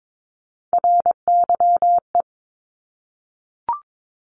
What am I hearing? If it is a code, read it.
Morse code, heard as a keyed tone, about 22 wpm, 700 Hz. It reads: RYE
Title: Multi-letter Phonemes 22wpm